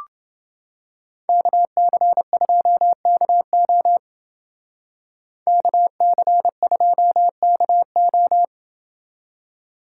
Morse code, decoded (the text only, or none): KC2KO KC2KO